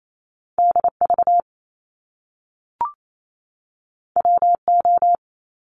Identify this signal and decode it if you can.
Morse code, heard as a keyed tone, about 28 wpm, 700 Hz. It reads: DV WO
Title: Top Two Letter Combinations in Words 28wpm